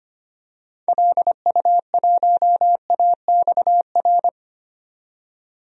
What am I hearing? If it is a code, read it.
Morse code, heard as a keyed tone, about 25 wpm, 700 Hz. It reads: LU1AXR
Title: DXCC-100 Call Signs - 6 Characters - 25wpm